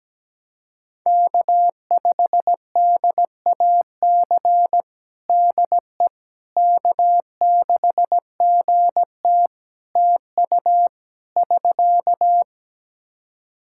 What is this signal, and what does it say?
Morse code, heard as a keyed tone, about 17 wpm, 700 Hz. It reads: K5DAC DE K6GT TU <SK>
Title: Practice QSO - Exchange 4 - Ending - 17wpm